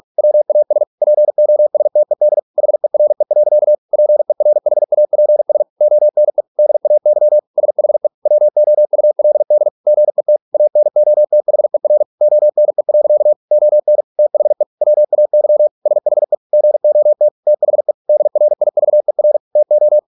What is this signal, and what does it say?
Morse code, heard as a keyed tone, about 46 wpm, 600 Hz. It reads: WAS POSTED HERE. PERHAPS ONE DAY SHE WOULD GET ANOTHER ONE. ON THE WAY SHE GOT THE DRIVER TO